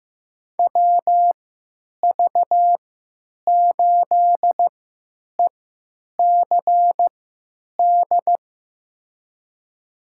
Morse code, heard as a keyed tone, about 15 wpm, spaced as for 9 wpm, 700 Hz. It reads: WV8ECD
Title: USA Call Signs - 6 Characters - 3x Character Spacing - 15wpm